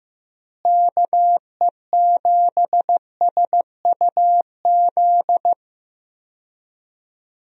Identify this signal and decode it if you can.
Morse code, heard as a keyed tone, about 16 wpm, 700 Hz. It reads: KE7SUZ
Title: USA Call Signs - 6 Characters - Repeated 2x - 15wpm